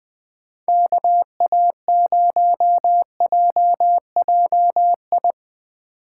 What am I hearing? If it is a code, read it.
Morse code, heard as a keyed tone, about 20 wpm, 700 Hz. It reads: KA0JJI
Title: Speed Racing - USA Call Signs 17wpm